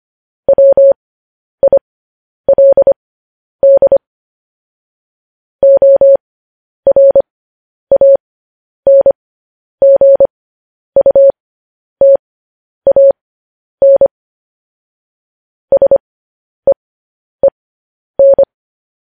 Morse code, heard as a keyed tone, about 25 wpm, spaced as for 10 wpm, 550 Hz. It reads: WILD ORANGUTAN SEEN